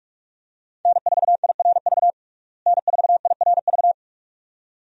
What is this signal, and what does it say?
Morse code, heard as a keyed tone, about 42 wpm, 700 Hz. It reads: N4IRV N4IRV
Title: USA Call Signs - 5 Characters - Repeated 2x - 45wpm